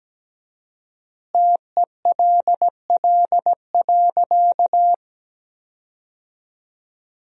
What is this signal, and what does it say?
Morse code, heard as a keyed tone, about 17 wpm, 700 Hz. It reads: TELL.
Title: 4x Word-Spacing - Sentences from Top 300 Words - 17wpm